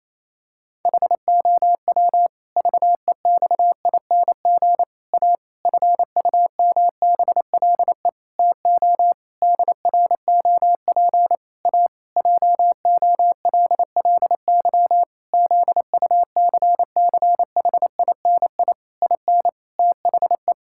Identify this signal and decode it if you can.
Morse code, heard as a keyed tone, about 27 wpm, 700 Hz. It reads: HOW VEXING A FUMBLE TO DROP A JOLLY ZUCCHINI IN THE
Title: Pangram Challenge v2 - 28wpm